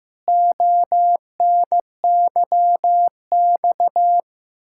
Morse code, heard as a keyed tone, about 16 wpm, 700 Hz. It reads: ONYX
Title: Pangram Challenge v2 - 15wpm